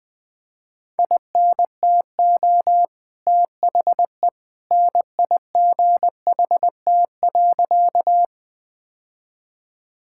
Morse code, heard as a keyed tone, about 20 wpm, 700 Hz. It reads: INTO THE NIGHT.
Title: Sentences with 2-Word Phrases - Spoken Beginning - 20wpm